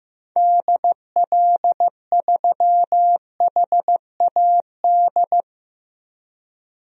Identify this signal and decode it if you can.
Morse code, heard as a keyed tone, about 15 wpm, 700 Hz. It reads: DL3HAD
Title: DXCC-100 Call Signs - 6 Characters - Repeated 2x - 15wpm